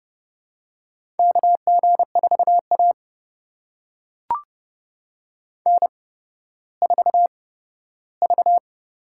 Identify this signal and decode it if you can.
Morse code, heard as a keyed tone, about 30 wpm, spaced as for 15 wpm, 700 Hz. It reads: KG4A N4V
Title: USA Call Signs - 4 Characters - 8x Character Spacing - 30wpm